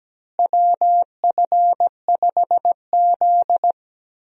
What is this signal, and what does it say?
Morse code, heard as a keyed tone, about 17 wpm, 700 Hz. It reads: WF5Z
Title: USA Call Signs - 4 Characters - Repeated 3x - 17wpm